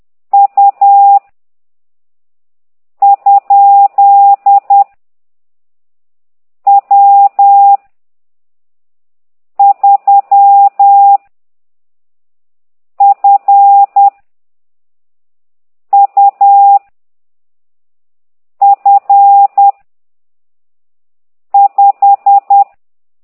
Morse code, heard as a keyed tone, about 10 wpm, 800 Hz. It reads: U ? W 3 F U F 5